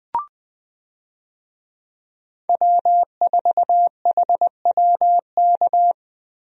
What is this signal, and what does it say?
Morse code, heard as a keyed tone, about 21 wpm, 700 Hz. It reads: W4HWK
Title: USA Call Signs - 5 Characters - Repeated 2x - 20wpm